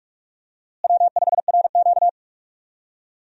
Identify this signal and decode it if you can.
Morse code, heard as a keyed tone, about 48 wpm, 700 Hz. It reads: W5RQ
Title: USA Call Signs - 4 Characters - Repeated 2x - 45wpm